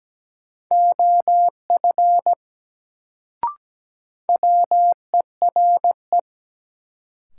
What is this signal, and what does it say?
Morse code, heard as a keyed tone, about 17 wpm, 700 Hz. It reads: OF WERE
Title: Sets of 2 Words - New Word Drill - 17wpm